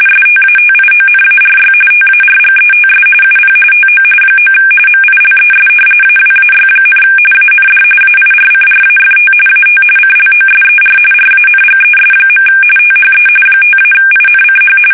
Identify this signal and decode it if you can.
RTTY, about 45.45 baud. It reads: STATIONARY TARGETS UP TO 9 MILES AWAY, AND AIRCRAFT AT CLOSER RANGES. THE 5-INCH
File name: fsk850 / battleshipnj